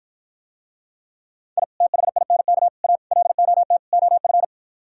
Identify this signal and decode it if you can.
Morse code, heard as a keyed tone, about 53 wpm, 700 Hz. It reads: I THINK A LOT OF